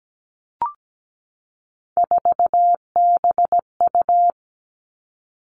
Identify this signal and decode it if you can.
Morse code, heard as a keyed tone, about 17 wpm, 700 Hz. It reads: 4BU
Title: Three Challenging Letter-Number Combinations 17wpm